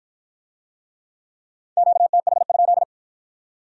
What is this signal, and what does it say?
Morse code, heard as a keyed tone, about 53 wpm, 700 Hz. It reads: QTH?